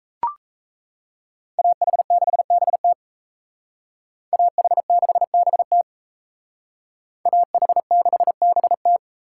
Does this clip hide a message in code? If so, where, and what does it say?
Yes: Morse code from 1.6 to 9.0 s, heard as a keyed tone, about 36 wpm, 700 Hz. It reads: AH6BT AH6BT AH6BT